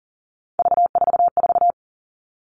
Morse code, heard as a keyed tone, about 38 wpm, 700 Hz. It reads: V44
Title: Three Challenging Letter-Number Combinations 40wpm